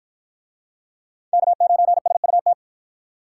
Morse code, heard as a keyed tone, about 53 wpm, 700 Hz. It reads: K9SUT